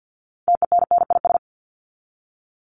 Morse code, heard as a keyed tone, about 50 wpm, 700 Hz. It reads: TENNIS
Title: Top 1501-1750 Words 50wpm